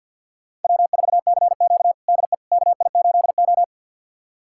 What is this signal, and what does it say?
Morse code, heard as a keyed tone, about 51 wpm, 700 Hz. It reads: W4CQ DE KI8O
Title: USA Callsign DE Callsign - 4 Characters - 50wpm